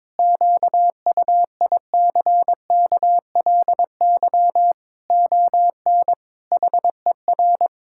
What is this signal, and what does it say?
Morse code, heard as a keyed tone, about 22 wpm, 700 Hz. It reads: QUICKLY ON HER
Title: Pangram Challenge v2 - 22wpm